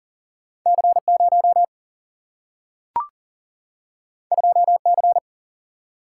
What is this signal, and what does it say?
Morse code, heard as a keyed tone, about 40 wpm, 700 Hz. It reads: C 0 2 C
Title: Single-Letter Single-Number 40wpm